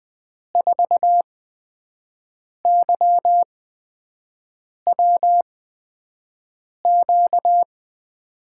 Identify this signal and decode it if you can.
Morse code, heard as a keyed tone, about 19 wpm, spaced as for 5 wpm, 700 Hz. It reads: 4YWQ
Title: USA Call Signs - 6 Characters - 8x Character Spacing - 20wpm